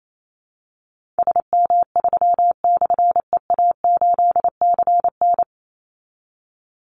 Morse code, heard as a keyed tone, about 28 wpm, 700 Hz. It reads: SM3/EA8CN
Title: Call Signs Repeated 2x - 28wpm